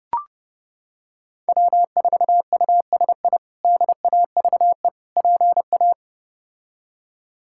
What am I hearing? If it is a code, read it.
Morse code, heard as a keyed tone, about 30 wpm, 700 Hz. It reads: W4USI DAVE PA
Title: K1USN SST QSO Practice - 30wpm